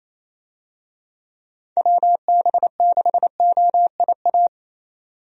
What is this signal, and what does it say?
Morse code, heard as a keyed tone, about 28 wpm, 700 Hz. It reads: WB6OIA
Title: USA Call Signs - 6 Characters - 28wpm